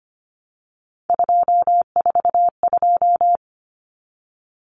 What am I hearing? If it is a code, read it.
Morse code, heard as a keyed tone, about 25 wpm, 700 Hz. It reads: 242
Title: Numbers - 3-Digit 25wpm